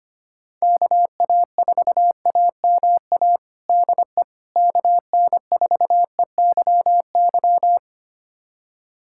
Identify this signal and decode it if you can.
Morse code, heard as a keyed tone, about 25 wpm, 700 Hz. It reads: KA4AMA DE KN4EYY
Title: USA Callsign DE Callsign 25wpm